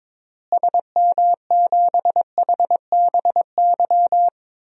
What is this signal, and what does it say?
Morse code, heard as a keyed tone, about 22 wpm, 700 Hz. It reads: SM7HBY
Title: DXCC-100 Call Signs - 6 Characters - Repeated 3x - 22wpm